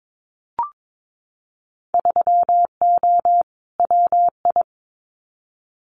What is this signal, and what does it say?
Morse code, heard as a keyed tone, about 22 wpm, 700 Hz. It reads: 3O WI
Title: Winter Field Day Exchanges 22wpm